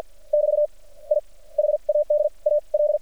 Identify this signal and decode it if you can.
Morse code, heard as a keyed tone, about 48 wpm, spaced as for 41 wpm, 600 Hz. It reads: <BT> I HAD A L
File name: archived / CRSnet